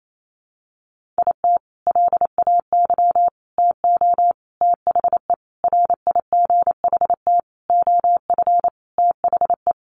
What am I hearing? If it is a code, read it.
Morse code, heard as a keyed tone, about 28 wpm, 700 Hz. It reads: IT LAY TO THE RIGHT OF THE
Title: Sentences from Top 1400 Words 28wpm